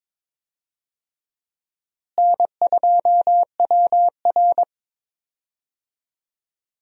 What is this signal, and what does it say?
Morse code, heard as a keyed tone, about 21 wpm, 700 Hz. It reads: N2WR
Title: USA Call Signs - 4 Characters - Repeated 3x - 22wpm